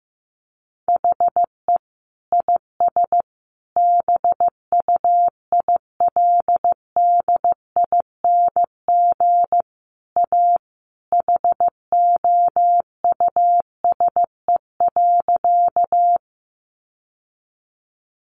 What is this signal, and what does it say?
Morse code, heard as a keyed tone, about 15 wpm, 700 Hz. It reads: HE IS BUILDING A HOUSE.
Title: Sentences from Top 700 Words 15wpm